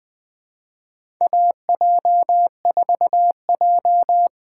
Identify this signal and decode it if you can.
Morse code, heard as a keyed tone, about 21 wpm, 700 Hz. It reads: AJ4J
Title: USA Call Signs - 4 Characters - 6x Character Spacing - 20wpm